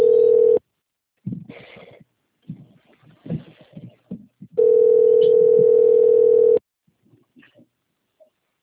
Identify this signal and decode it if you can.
Ringback tone (North America), 440 + 480 Hz.
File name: rg-1600-unknown- - - .20561